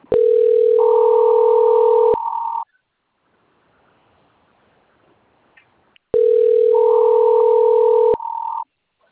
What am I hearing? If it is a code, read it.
Ringback tone (North America), 440 + 480 Hz.